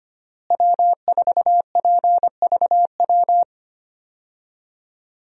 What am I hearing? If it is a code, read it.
Morse code, heard as a keyed tone, about 25 wpm, 700 Hz. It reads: W4PVW
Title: USA Call Signs - 5 Characters - Repeated 3x - 25wpm